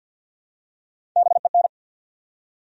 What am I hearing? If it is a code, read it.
Morse code, heard as a keyed tone, about 48 wpm, 700 Hz. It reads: BEN
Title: USA Common Names - Max 5 Letters - 50wpm